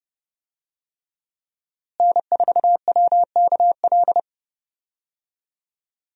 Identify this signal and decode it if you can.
Morse code, heard as a keyed tone, about 30 wpm, 700 Hz. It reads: N4WKL
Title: USA Call Signs - 5 Characters - Repeated 3x - 30wpm